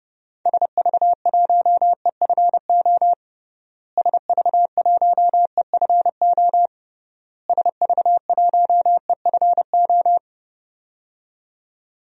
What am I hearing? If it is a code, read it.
Morse code, heard as a keyed tone, about 31 wpm, 700 Hz. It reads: SV1EFO SV1EFO SV1EFO
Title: DXCC-100 Call Signs - 6 Characters - Repeated 3x - 30wpm